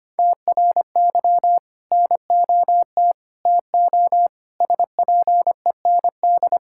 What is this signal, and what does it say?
Morse code, heard as a keyed tone, about 25 wpm, 700 Hz. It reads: TRY NOT TO SPEND
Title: Sentences from Top 1000 Words - Review Part 5 - 25wpm